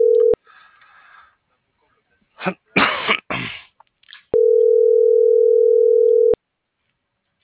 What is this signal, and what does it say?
Ringback tone (North America), 440 + 480 Hz.